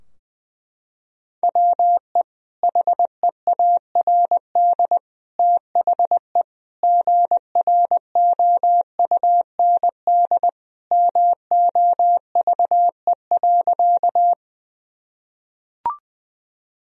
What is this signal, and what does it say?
Morse code, heard as a keyed tone, about 20 wpm, 700 Hz. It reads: WE HEARD THE GROUND MOVE.
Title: Sentences from Top 1000 Words - Review Part 2 - 20wpm